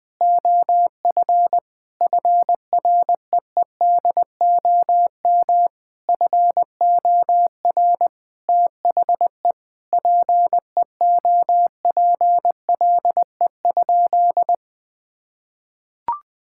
Morse code, heard as a keyed tone, about 20 wpm, 700 Hz. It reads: OF FREEDOM FOR THE PEOPLE?
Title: Sentences from Top 1000 Words - Review Part 5 - 20wpm